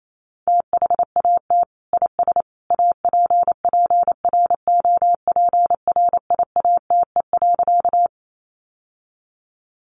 Morse code, heard as a keyed tone, about 28 wpm, 700 Hz. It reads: THAT IS APPROPRIATE.
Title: Sentences from Top 900 Words 28wpm